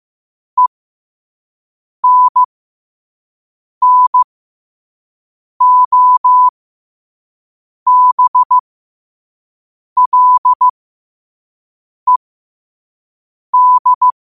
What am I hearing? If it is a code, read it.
Morse code, heard as a keyed tone, about 15 wpm, spaced as for 5 wpm, 1000 Hz. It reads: ENNOBLED
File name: qotdfiles / 5wpm